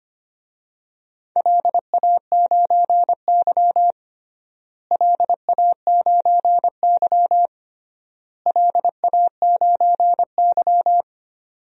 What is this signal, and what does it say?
Morse code, heard as a keyed tone, about 25 wpm, 700 Hz. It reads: LA9Y LA9Y LA9Y
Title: DXCC-100 Call Signs - 4 Characters - Repeated 3x - 25wpm